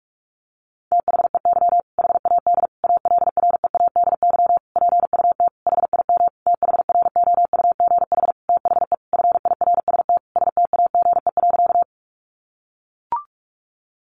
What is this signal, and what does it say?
Morse code, heard as a keyed tone, about 44 wpm, 700 Hz. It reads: THEY HAD ALREADY PUT HIM THROUGH THE FIRST STAGE.
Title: Sentences from Top 1200 Words 45wpm